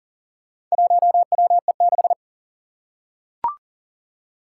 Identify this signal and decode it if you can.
Morse code, heard as a keyed tone, about 40 wpm, 700 Hz. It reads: 1WE6
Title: Four Challenging Letter-Number Combinations 40wpm